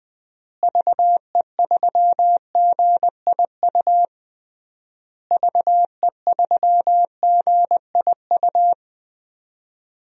Morse code, heard as a keyed tone, about 20 wpm, 700 Hz. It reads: VE3GIU VE3GIU
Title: DXCC-100 Call Signs - 6 Characters - Repeated 3x - 20wpm